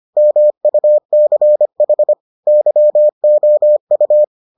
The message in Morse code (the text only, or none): MUCH YOU